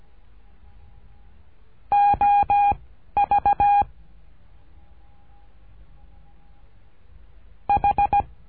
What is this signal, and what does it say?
Morse code, heard as a keyed tone, about 17 wpm, 810 Hz. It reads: O V H
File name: cursos / CURSO DE TELEGRAFIA / Telegrafia - Trilha 13